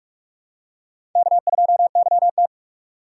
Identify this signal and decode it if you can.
Morse code, heard as a keyed tone, about 41 wpm, 700 Hz. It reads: K2YT